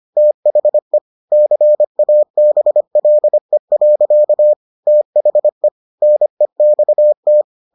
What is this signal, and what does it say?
Morse code, heard as a keyed tone, about 25 wpm, 600 Hz. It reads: THE CABLE. THE NEXT